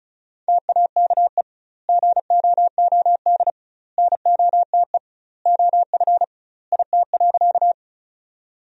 Morse code, heard as a keyed tone, about 34 wpm, 700 Hz. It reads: TAKE GOOD NOTE OF IT.
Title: 2x Word-Spacing - Sentences from Top 500 Words - 35wpm